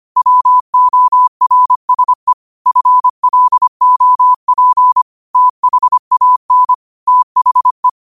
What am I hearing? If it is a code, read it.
Morse code, heard as a keyed tone, about 25 wpm, 1000 Hz. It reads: WORSE FLOP THAN THE
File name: qotdfiles / 25wpm